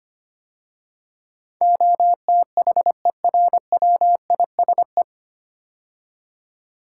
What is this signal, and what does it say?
Morse code, heard as a keyed tone, about 25 wpm, 700 Hz. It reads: OTHERWISE
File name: practice / 25wpm / 9 Letter Words - First 3 Letters Given - 25wpm